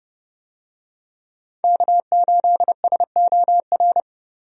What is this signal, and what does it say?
Morse code, heard as a keyed tone, about 31 wpm, 700 Hz. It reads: K8SOR
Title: USA Call Signs - 5 Characters - 30wpm